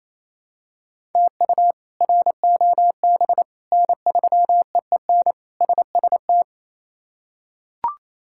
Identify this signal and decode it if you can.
Morse code, heard as a keyed tone, about 28 wpm, 700 Hz. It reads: TU ROB N3EEN SST
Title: K1USN SST QSO Practice - 28wpm